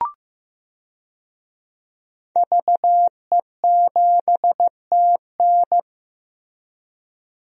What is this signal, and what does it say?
Morse code, heard as a keyed tone, about 15 wpm, 700 Hz. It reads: VE7TN
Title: DXCC-100 Call Signs - 5 Characters - Repeated 3x - 15wpm